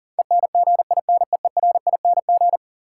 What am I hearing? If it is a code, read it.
Morse code, heard as a keyed tone, about 40 wpm, 700 Hz. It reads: ENGINEERING
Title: Sentences from Top 1500 Words - New Word Drill - 40wpm